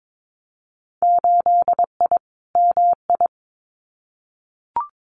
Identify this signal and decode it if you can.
Morse code, heard as a keyed tone, about 22 wpm, 700 Hz. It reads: 8I MI
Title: Winter Field Day Exchanges Encore 22wpm